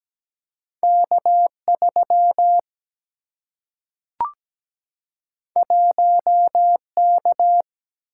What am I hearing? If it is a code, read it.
Morse code, heard as a keyed tone, about 17 wpm, 700 Hz. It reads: K3 1K
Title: Single-Letter Single-Number 17wpm